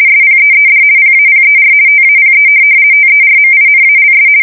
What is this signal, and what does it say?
RTTY, about 50 baud. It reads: SPEED PAPER TAPE TERMINALS
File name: fsk170-50B / teletype-corp